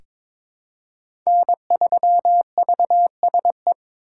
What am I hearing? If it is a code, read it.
Morse code, heard as a keyed tone, about 22 wpm, 700 Hz. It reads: N3VSE
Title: USA Call Signs - 5 Characters - 4x Character Spacing - 22wpm